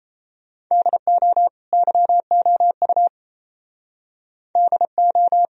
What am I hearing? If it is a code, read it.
Morse code, heard as a keyed tone, about 36 wpm, 700 Hz. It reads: DO YOU DO